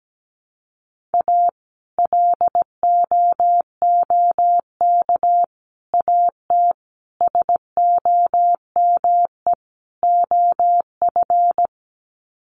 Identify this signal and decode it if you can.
Morse code, heard as a keyed tone, about 17 wpm, 700 Hz. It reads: A LOOK AT SOME OF
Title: Sets of 5 Words Encore2 17wpm